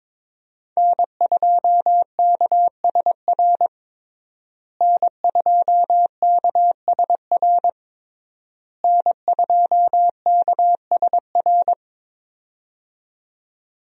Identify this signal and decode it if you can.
Morse code, heard as a keyed tone, about 21 wpm, 700 Hz. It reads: N2KSR N2KSR N2KSR
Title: USA Call Signs - 5 Characters - Repeated 3x - 22wpm